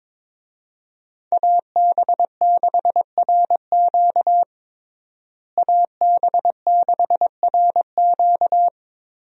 Morse code, heard as a keyed tone, about 22 wpm, 700 Hz. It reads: AB6RQ AB6RQ
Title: USA Call Signs - 5 Characters - Repeated 3x - 22wpm